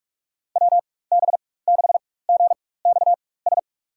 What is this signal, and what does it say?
Morse code, heard as a keyed tone, about 46 wpm, spaced as for 21 wpm, 700 Hz. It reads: WB6GXS
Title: USA Call Signs - 6 Characters - 4x Character Spacing - 45wpm